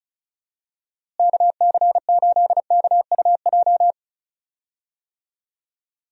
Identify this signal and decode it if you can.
Morse code, heard as a keyed tone, about 34 wpm, 700 Hz. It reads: KC8KUJ
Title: USA Call Signs - 6 Characters - Repeated 3x - 35wpm